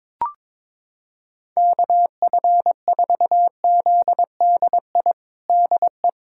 Morse code, heard as a keyed tone, about 22 wpm, 700 Hz. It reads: KF4ZDI DE
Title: USA Callsign DE Callsign - 6 Characters - 22wpm